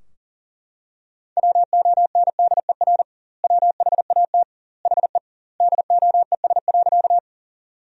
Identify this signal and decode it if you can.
Morse code, heard as a keyed tone, about 40 wpm, 700 Hz. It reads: WONDER WHAT HE DOES.
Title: 2x Word-Spacing - Sentences from Top 500 Words - 40wpm